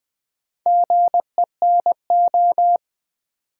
Morse code, heard as a keyed tone, about 20 wpm, 700 Hz. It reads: GENO
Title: Build A Word - 2 Letters At A Time - 8-Letter Words - 20wpm